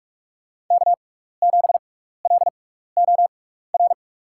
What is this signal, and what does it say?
Morse code, heard as a keyed tone, about 46 wpm, spaced as for 16 wpm, 700 Hz. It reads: K7LOR
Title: USA Call Signs - 5 Characters - 6x Character Spacing - 45wpm